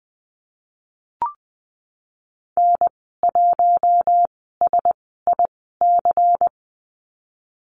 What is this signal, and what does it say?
Morse code, heard as a keyed tone, about 20 wpm, spaced as for 15 wpm, 700 Hz. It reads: N1SIC
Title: 122 Lesson 20g - Number 3 - Call Signs 20wpm